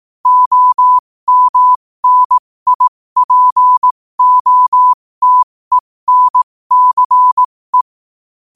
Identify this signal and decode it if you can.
Morse code, heard as a keyed tone, about 18 wpm, spaced as for 15 wpm, 1000 Hz. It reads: OMNIPOTENCE
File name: qotdfiles / 15wpm